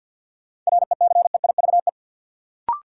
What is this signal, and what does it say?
Morse code, heard as a keyed tone, about 49 wpm, 700 Hz. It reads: RECEIVE
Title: Sentences from Top 1000 Words - New Word Drill - 50wpm